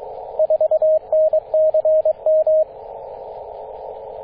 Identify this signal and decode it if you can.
Morse code, heard as a keyed tone, about 22 wpm, 630 Hz, with radio noise. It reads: 4NCM
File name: 9m0o-24-891mhz-cw